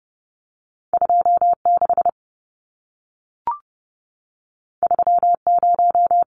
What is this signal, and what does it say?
Morse code, heard as a keyed tone, about 30 wpm, 700 Hz. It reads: 26 30
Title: Numbers - 2-Digit 30wpm